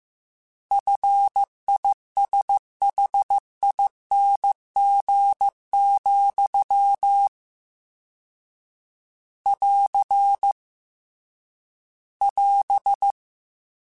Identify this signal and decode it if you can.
Morse code, heard as a keyed tone, about 15 wpm, 800 Hz. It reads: FISHING, <AR> <AS>
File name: Combinations / Combinations-mp3s / Combinations@13